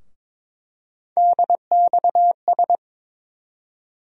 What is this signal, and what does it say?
Morse code, heard as a keyed tone, about 22 wpm, 700 Hz. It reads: DXS
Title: USA Call Sign Postfixes - 3 Letters 22wpm